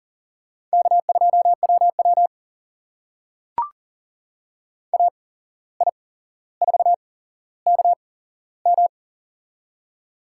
Morse code, heard as a keyed tone, about 40 wpm, spaced as for 16 wpm, 700 Hz. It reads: K2WW AI4KM
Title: USA Call Signs - 8x Character Spacing - 40wpm